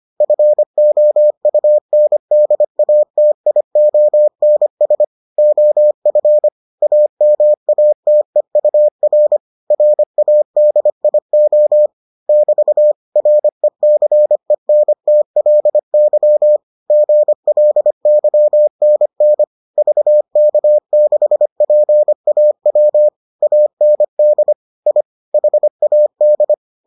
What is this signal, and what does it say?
Morse code, heard as a keyed tone, about 25 wpm, 600 Hz. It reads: FOUNDATIONS OF AMATEUR RADIO <BT> RECENTLY GLYNN VK6PAW AND I HAD